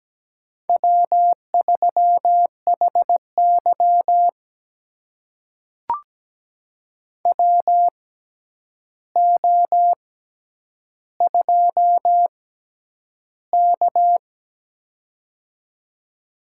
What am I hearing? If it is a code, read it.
Morse code, heard as a keyed tone, about 17 wpm, spaced as for 9 wpm, 700 Hz. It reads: W3HY WO2K
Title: USA Call Signs - 4 Characters - 6x Character Spacing - 17wpm